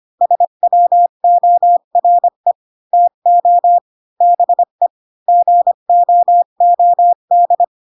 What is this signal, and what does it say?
Morse code, heard as a keyed tone, about 25 wpm, 700 Hz. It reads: SWORE TO BE GOOD